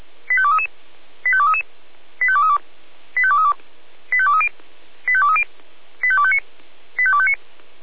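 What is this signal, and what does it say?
Five-tone selective call (ZVEI), heard as quick runs of five tones, tones of about 70 ms. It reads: 85233 85233 85232 85232 85230 85230 85259 85259
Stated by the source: ZVEI five-tone call received by SDR, recorded by Feuerwehr Orlamünde.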